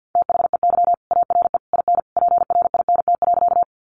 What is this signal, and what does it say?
Morse code, heard as a keyed tone, about 50 wpm, 700 Hz. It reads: THEY ARE IN PRINT.
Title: Sentences from Top 1400 Words 50wpm